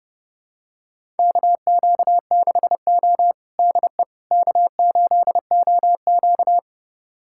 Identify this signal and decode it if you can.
Morse code, heard as a keyed tone, about 30 wpm, 700 Hz. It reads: KQ6O DE K8OQ
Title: USA Callsign DE Callsign - 4 Characters - 30wpm